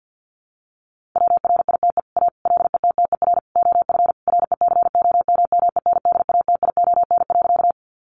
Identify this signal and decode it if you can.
Morse code, heard as a keyed tone, about 50 wpm, 700 Hz. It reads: WRITE A LETTER OF RECOMMENDATION.
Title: Sentences from Top 1400 Words 50wpm